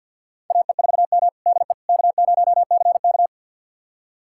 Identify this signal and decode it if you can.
Morse code, heard as a keyed tone, about 52 wpm, 700 Hz. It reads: AE4M DE K0CK
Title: USA Callsign DE Callsign - 4 Characters - 50wpm